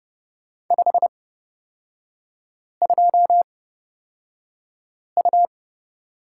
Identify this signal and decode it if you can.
Morse code, heard as a keyed tone, about 30 wpm, spaced as for 5 wpm, 700 Hz. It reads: 52U
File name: Lesson 36e - BT - Three Characters 30wpm